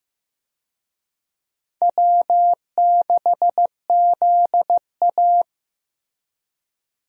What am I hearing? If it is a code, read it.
Morse code, heard as a keyed tone, about 16 wpm, 700 Hz. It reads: W6ZA
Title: USA Call Signs - 4 Characters - Repeated 2x - 15wpm